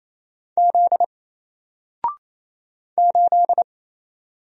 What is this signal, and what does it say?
Morse code, heard as a keyed tone, about 28 wpm, 700 Hz. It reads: Z 8
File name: Single-Letter-Number-28wpm